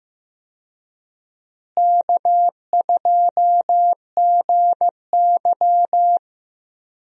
Morse code, heard as a keyed tone, about 15 wpm, 700 Hz. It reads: K2GY
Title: USA Call Signs - 4 Characters - Repeated 2x - 15wpm